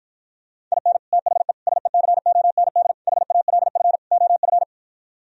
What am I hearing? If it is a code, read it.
Morse code, heard as a keyed tone, about 52 wpm, 700 Hz. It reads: IN THE SECOND HALF OF